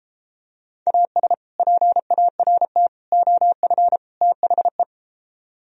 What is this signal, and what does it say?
Morse code, heard as a keyed tone, about 32 wpm, 700 Hz. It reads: AS PART OF THE